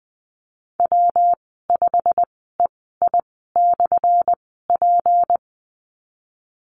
Morse code, heard as a keyed tone, about 19 wpm, spaced as for 14 wpm, 700 Hz. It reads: W5EI/P
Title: 179 Lesson 29g - Slash - Call Signs 20wpm